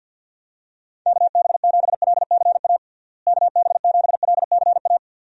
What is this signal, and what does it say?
Morse code, heard as a keyed tone, about 50 wpm, 700 Hz. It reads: KB7LCA KB7LCA
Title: USA Call Signs - 6 Characters - Repeated 2x - 50wpm